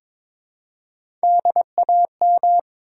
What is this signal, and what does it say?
Morse code, heard as a keyed tone, about 22 wpm, 700 Hz. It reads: DAM
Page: morse-code-ninja-practice-22wpm